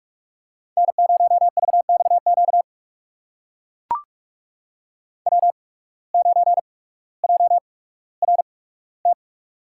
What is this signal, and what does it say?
Morse code, heard as a keyed tone, about 44 wpm, spaced as for 19 wpm, 700 Hz. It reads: N0VXQ W9JRT
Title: USA Call Signs - 5 Characters - 8x Character Spacing - 45wpm